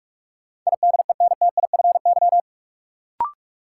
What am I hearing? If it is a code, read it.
Morse code, heard as a keyed tone, about 46 wpm, 700 Hz. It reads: IDENTIFY
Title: Sentences from Top 900 Words - New Word Drill - 45wpm